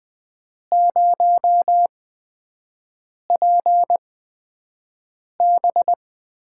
Morse code, heard as a keyed tone, about 20 wpm, spaced as for 5 wpm, 700 Hz. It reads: 0PB
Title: USA Call Signs - 8x Character Spacing - 20wpm